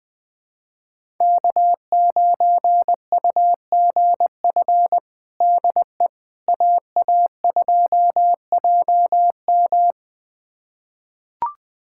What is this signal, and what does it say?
Morse code, heard as a keyed tone, about 20 wpm, 700 Hz. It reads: K9UGF DE AA2JM
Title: USA Callsign DE Callsign - 5 Characters - 20wpm